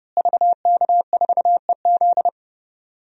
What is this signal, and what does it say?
Morse code, heard as a keyed tone, about 30 wpm, 700 Hz. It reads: VK4EZ
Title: DXCC-100 Call Signs - 5 Characters - Repeated 3x - 30wpm